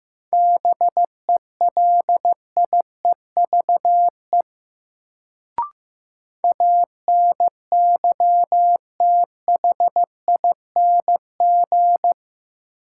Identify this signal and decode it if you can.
Morse code, heard as a keyed tone, about 15 wpm, 700 Hz. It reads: BELIEVE ANYTHING
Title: Sets of 4 Words - New Word Drill - 15wpm